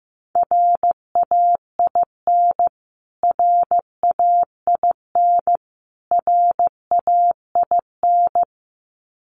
Morse code, heard as a keyed tone, about 15 wpm, 700 Hz. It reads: RAIN RAIN RAIN
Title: Top 500 Words Repeated 15wpm